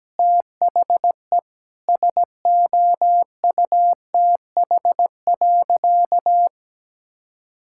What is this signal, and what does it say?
Morse code, heard as a keyed tone, about 17 wpm, 700 Hz. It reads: THE SOUTH.
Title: Sentences from Top 1000 Words - Review Part 6 - 17wpm